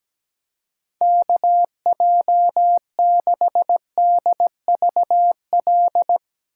Morse code, heard as a keyed tone, about 18 wpm, 700 Hz. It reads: KJ6DVL
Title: Speed Racing - USA Call Signs 17wpm